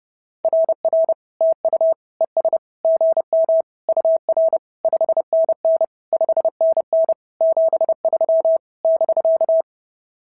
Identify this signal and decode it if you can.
Morse code, heard as a keyed tone, about 30 wpm, 690 Hz. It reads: RR TU ES GM UR 5NN 5NN 73 <BK>
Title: SOTA QSO Practice 30wpm